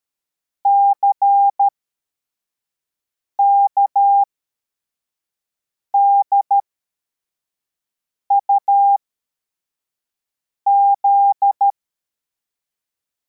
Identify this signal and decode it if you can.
Morse code, heard as a keyed tone, about 13 wpm, spaced as for 4 wpm, 800 Hz. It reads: CKDUZ